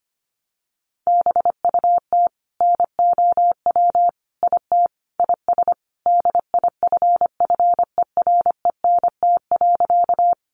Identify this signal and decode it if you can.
Morse code, heard as a keyed tone, about 25 wpm, 700 Hz. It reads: BUT NOW IT IS DIFFERENT.
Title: Sentences from Top 600 Words 25wpm